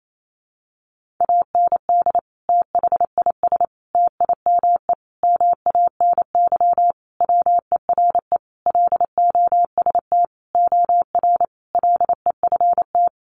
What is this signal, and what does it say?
Morse code, heard as a keyed tone, about 28 wpm, 700 Hz. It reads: AND THIS TIME MANY WERE LOST OR LEFT
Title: Sentences from Top 1400 Words 28wpm